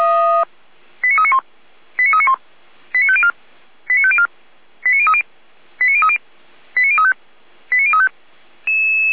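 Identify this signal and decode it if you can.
Five-tone selective call (ZVEI), heard as quick runs of five tones, tones of about 70 ms. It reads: 89391 89391 89504 89504 89030 89030 89035 89035
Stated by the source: Five-tone ZVEI selective call 89125, received on SDR.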